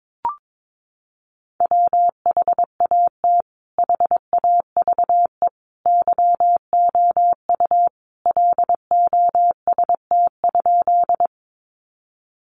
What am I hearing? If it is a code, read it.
Morse code, heard as a keyed tone, about 22 wpm, 700 Hz. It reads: WHAT HAVE YOU LOST?
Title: Sentences from Top 1400 Words 22wpm